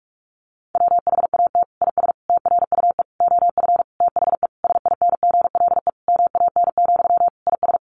Morse code, heard as a keyed tone, about 44 wpm, 700 Hz. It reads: WHAT IS TRUE OF THE SINGLE MAN, IS
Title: Sentences from Top 1400 Words 45wpm